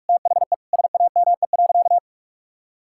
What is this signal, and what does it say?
Morse code, heard as a keyed tone, about 46 wpm, 700 Hz. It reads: THE SAME.
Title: Sentences from Top 1600 Words 45wpm